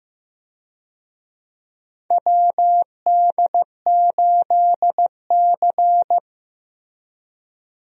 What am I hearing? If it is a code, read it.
Morse code, heard as a keyed tone, about 15 wpm, 700 Hz. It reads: WD8C
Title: USA Call Signs - 4 Characters - Repeated 2x - 15wpm